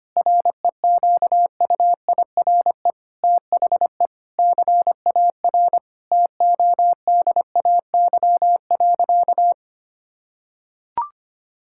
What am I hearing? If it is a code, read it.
Morse code, heard as a keyed tone, about 25 wpm, 700 Hz. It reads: REQUIRE THE CAR TODAY.
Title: Sentences from Top 1000 Words - Review Part 6 - 25wpm